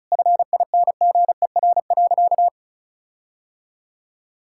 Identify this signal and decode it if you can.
Morse code, heard as a keyed tone, about 36 wpm, 700 Hz. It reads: FINGER.
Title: Sentences from Top 1600 Words 35wpm